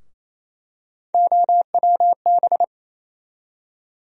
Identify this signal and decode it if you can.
Morse code, heard as a keyed tone, about 28 wpm, 700 Hz. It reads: OWB
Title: USA Call Sign Postfixes - 3 Letters 28wpm